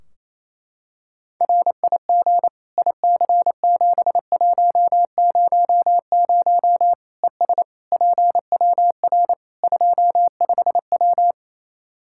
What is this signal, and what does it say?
Morse code, heard as a keyed tone, about 28 wpm, 700 Hz. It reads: RIG IC7100 ES PWR 25W
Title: K1USN SST QSO Practice 28wpm